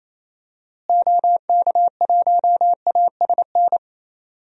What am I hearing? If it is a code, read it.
Morse code, heard as a keyed tone, about 28 wpm, 700 Hz. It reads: OK1ASN
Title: DXCC-100 Call Signs - 6 Characters - Repeated 3x - 28wpm